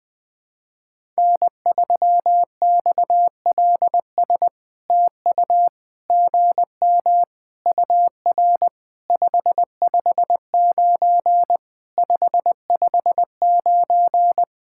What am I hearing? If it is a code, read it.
Morse code, heard as a keyed tone, about 20 wpm, 700 Hz. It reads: N3XLS TU GM UR 559 559
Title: POTA QSO Practice - 20wpm